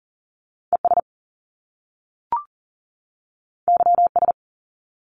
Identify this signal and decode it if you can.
Morse code, heard as a keyed tone, about 40 wpm, 700 Hz. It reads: ES YS
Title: Top Two Letter Combinations in Words 40wpm